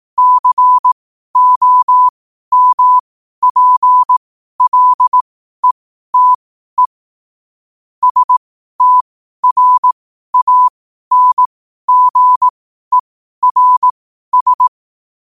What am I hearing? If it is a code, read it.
Morse code, heard as a keyed tone, about 18 wpm, spaced as for 12 wpm, 1000 Hz. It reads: COMPLETE STRANGERS
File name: qotdfiles / 12wpm